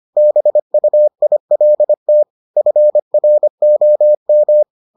Morse code, heard as a keyed tone, about 25 wpm, 600 Hz. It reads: BUILT FROM